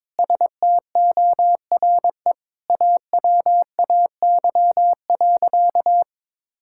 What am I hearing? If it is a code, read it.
Morse code, heard as a keyed tone, about 22 wpm, 700 Hz. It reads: STORE AWAY.
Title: Sentences from Top 1000 Words - Review Part 5 - 22wpm